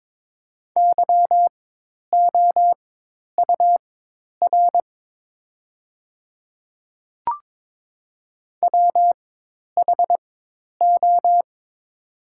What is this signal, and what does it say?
Morse code, heard as a keyed tone, about 22 wpm, spaced as for 10 wpm, 700 Hz. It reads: YOUR WHO
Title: 4x Character Spacing - Top 100 Words - 22wpm